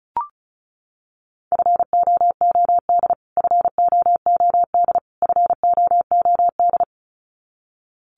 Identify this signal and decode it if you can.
Morse code, heard as a keyed tone, about 34 wpm, 700 Hz. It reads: FOOD FOOD FOOD
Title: Top 500 Words Repeated 35wpm